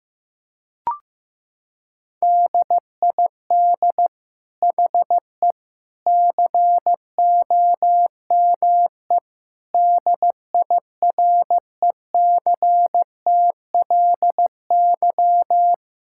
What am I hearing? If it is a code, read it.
Morse code, heard as a keyed tone, about 15 wpm, 700 Hz. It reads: DID HE COME DIRECTLY
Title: Sentences from Top 1000 Words - Review Part 5 - 15wpm